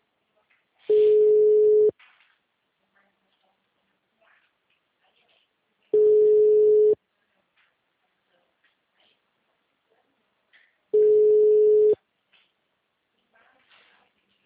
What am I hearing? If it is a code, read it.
Ringback tone (Europe, 425 Hz), with speech.